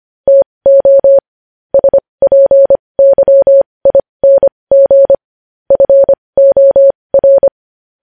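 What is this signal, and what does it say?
Morse code, heard as a keyed tone, about 25 wpm, spaced as for 20 wpm, 550 Hz. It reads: TO SPYING FOR
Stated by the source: Morse code news headlines sent at 20 wpm.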